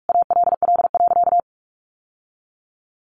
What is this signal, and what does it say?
Morse code, heard as a keyed tone, about 44 wpm, 700 Hz. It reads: ALL.
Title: Sentences from Top 1400 Words 45wpm